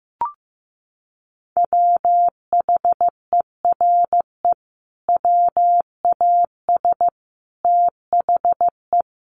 Morse code, heard as a keyed tone, about 15 wpm, 700 Hz. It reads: WHERE WAS THE
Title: Sentences from Top 1500 Words 15wpm